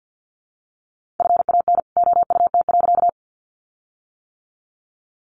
Morse code, heard as a keyed tone, about 50 wpm, 700 Hz. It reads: FAN OUT.
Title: Sentences from Top 1300 Words 50wpm